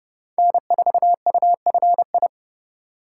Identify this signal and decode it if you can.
Morse code, heard as a keyed tone, about 36 wpm, 700 Hz. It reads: N4UFI
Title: Speed Racing - USA 1x3 Call Signs 30wpm